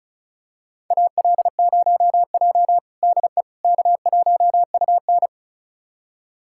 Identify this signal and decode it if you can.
Morse code, heard as a keyed tone, about 36 wpm, 700 Hz. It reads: AL0J DE K1UN
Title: USA Callsign DE Callsign - 4 Characters - 35wpm